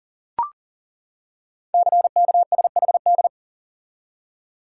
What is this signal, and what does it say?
Morse code, heard as a keyed tone, about 38 wpm, 700 Hz. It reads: CKSHD
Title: Code Groups - 5 Letters - 40wpm